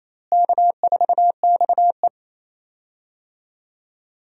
Morse code, heard as a keyed tone, about 28 wpm, 700 Hz. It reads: K4XE
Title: USA Call Signs - 4 Characters - Repeated 2x - 28wpm